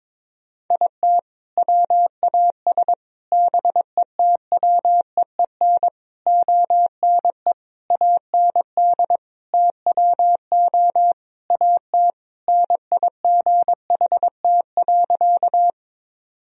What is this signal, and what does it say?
Morse code, heard as a keyed tone, about 22 wpm, 700 Hz. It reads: IT WAS BETWEEN ONE AND TWO AT NIGHT.
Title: Sentences from Top 1000 Words - Review Part 1 - 22wpm